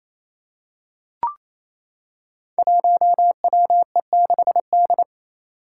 Morse code, heard as a keyed tone, about 28 wpm, 700 Hz. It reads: 1WE6D
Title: Five Challenging Letter-Number Combinations 28wpm